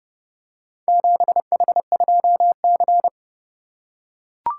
Morse code, heard as a keyed tone, about 31 wpm, 700 Hz. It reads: 7H2C